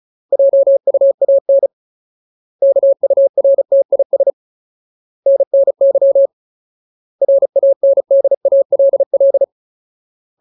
Morse code, heard as a keyed tone, about 34 wpm, 550 Hz. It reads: JUAN KURTIS NNY RANDALL
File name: Ultimate Word List / 35WPM Ultimate